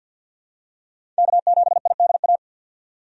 Morse code, heard as a keyed tone, about 53 wpm, 700 Hz. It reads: K7IDA
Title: USA Call Signs - 5 Characters - 50wpm